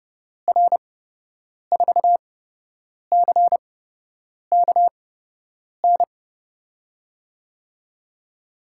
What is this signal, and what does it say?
Morse code, heard as a keyed tone, about 30 wpm, spaced as for 8 wpm, 700 Hz. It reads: R4CKN
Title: USA Call Signs - 6 Characters - 8x Character Spacing - 30wpm